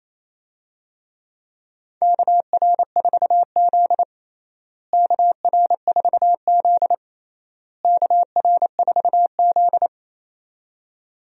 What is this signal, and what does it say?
Morse code, heard as a keyed tone, about 28 wpm, 700 Hz. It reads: KR4Z KR4Z KR4Z
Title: USA Call Signs - 4 Characters - Repeated 3x - 28wpm